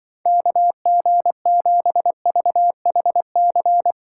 Morse code, heard as a keyed tone, about 24 wpm, 700 Hz. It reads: KG7VHC
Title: Speed Racing - USA 2x3 Call Signs 20wpm